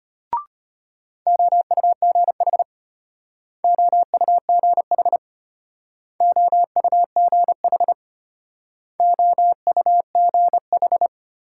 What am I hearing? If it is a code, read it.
Morse code, heard as a keyed tone, about 34 wpm, 700 Hz. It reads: OUGH OUGH OUGH OUGH